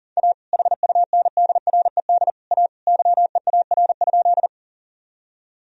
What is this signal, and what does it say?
Morse code, heard as a keyed tone, about 40 wpm, 700 Hz. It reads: A HUNDRED A YEAR?
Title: Sentences from Top 1000 Words - Review Part 2 - 40wpm